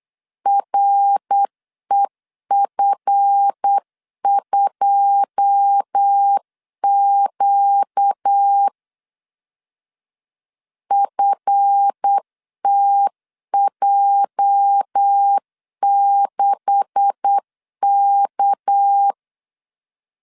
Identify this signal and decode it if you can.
Morse code, heard as a keyed tone, about 8 wpm, 800 Hz. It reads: REF2Q FTJ6K
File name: LEZIONI / 9 286 8wpm 3425